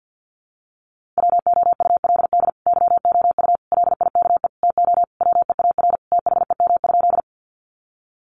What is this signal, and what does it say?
Morse code, heard as a keyed tone, about 50 wpm, 700 Hz. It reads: WOULD YOU LIKE TO WEAR THEM?
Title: Sentences from Top 1500 Words 50wpm